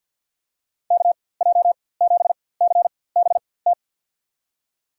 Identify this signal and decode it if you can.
Morse code, heard as a keyed tone, about 49 wpm, spaced as for 23 wpm, 700 Hz. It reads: KJ7CBT
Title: USA Call Signs - 4x Character Spacing - 50wpm